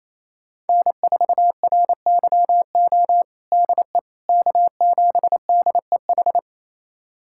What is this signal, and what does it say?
Morse code, heard as a keyed tone, about 28 wpm, 700 Hz. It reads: N4RYO DE K7DEH
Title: USA Callsign DE Callsign - 5 Characters - 28wpm